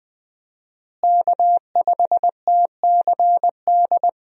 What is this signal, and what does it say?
Morse code, heard as a keyed tone, about 20 wpm, 700 Hz. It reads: K5TCD
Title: USA Call Signs - 5 Characters - Repeated 2x - 20wpm